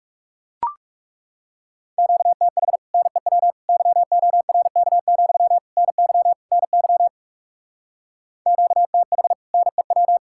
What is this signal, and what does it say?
Morse code, heard as a keyed tone, about 36 wpm, 700 Hz. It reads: QTH NEW YORK, NY NY QTH NEW
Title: Speed-Racing Common QSO Phrases Encore - 30wpm